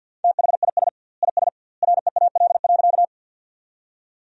Morse code, heard as a keyed tone, about 53 wpm, 700 Hz. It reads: THIS IS REAL.
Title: 2x Word-Spacing - Sentences from Top 300 Words - 50wpm